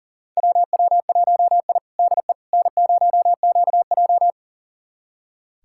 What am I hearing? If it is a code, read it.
Morse code, heard as a keyed tone, about 38 wpm, 700 Hz. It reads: WW1I DE N0QJ
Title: USA Callsign DE Callsign - 4 Characters - 40wpm